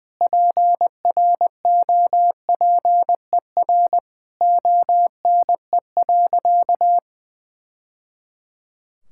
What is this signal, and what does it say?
Morse code, heard as a keyed tone, about 20 wpm, 700 Hz. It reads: PROPER ONE.
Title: Sentences from Top 1000 Words - Review Part 5 - 20wpm